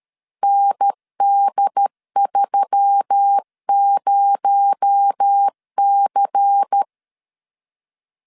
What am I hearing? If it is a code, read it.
Morse code, heard as a keyed tone, about 13 wpm, 800 Hz. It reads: ND30C
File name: LEZIONI / 8 YJ45 / 8 Y 12wpm 2228